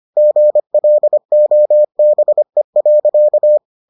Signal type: Morse code

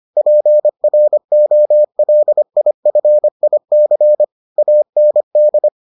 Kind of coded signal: Morse code

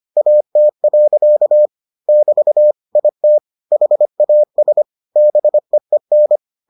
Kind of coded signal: Morse code

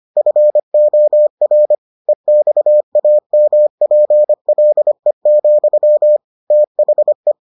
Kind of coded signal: Morse code

